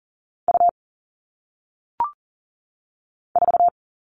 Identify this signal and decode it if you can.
Morse code, heard as a keyed tone, about 40 wpm, 700 Hz. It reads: U 4
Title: All Cut Numbers - 1-Digit 40wpm